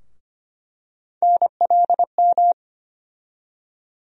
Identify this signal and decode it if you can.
Morse code, heard as a keyed tone, about 25 wpm, 700 Hz. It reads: NLM
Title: USA Call Sign Postfixes - 3 Letters 25wpm